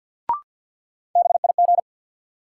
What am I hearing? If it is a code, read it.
Morse code, heard as a keyed tone, about 50 wpm, 700 Hz. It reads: BIG
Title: Sentences from Top 200 Words - New Word Drill - 50wpm